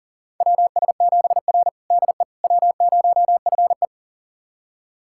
Morse code, heard as a keyed tone, about 40 wpm, 700 Hz. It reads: WS7R DE W0FE
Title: USA Callsign DE Callsign - 4 Characters - 40wpm